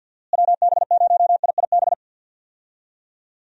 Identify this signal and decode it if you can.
Morse code, heard as a keyed tone, about 51 wpm, 700 Hz. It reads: WB0IIB
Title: USA Call Signs - 6 Characters - Repeated 3x - 50wpm